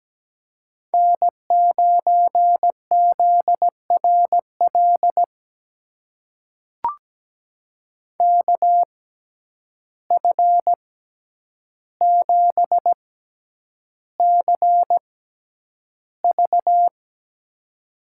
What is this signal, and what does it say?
Morse code, heard as a keyed tone, about 17 wpm, spaced as for 9 wpm, 700 Hz. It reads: N9ZRL KF7CV
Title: USA Call Signs - 5 Characters - 6x Character Spacing - 17wpm